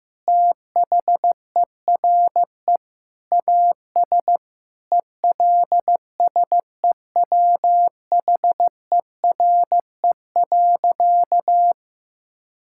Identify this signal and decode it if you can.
Morse code, heard as a keyed tone, about 15 wpm, 700 Hz. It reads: THERE AS ELSEWHERE.
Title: Sentences from Top 1600 Words 15wpm